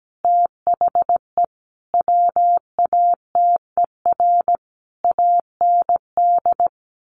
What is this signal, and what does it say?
Morse code, heard as a keyed tone, about 17 wpm, 700 Hz. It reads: THE WATER AND
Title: Sentences from Top 1300 Words 17wpm